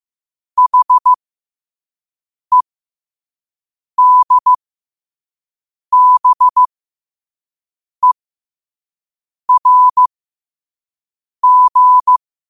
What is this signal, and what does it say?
Morse code, heard as a keyed tone, about 15 wpm, spaced as for 5 wpm, 1000 Hz. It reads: HEDBERG
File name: qotdfiles / 5wpm